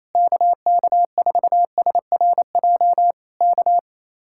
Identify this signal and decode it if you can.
Morse code, heard as a keyed tone, about 27 wpm, 700 Hz. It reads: KK4SRJ K
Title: Speed-Racing Common QSO Phrases Encore - 28wpm